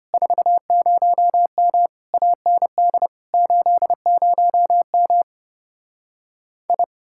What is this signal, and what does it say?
Morse code, heard as a keyed tone, about 33 wpm, 700 Hz. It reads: 40M AND 80M I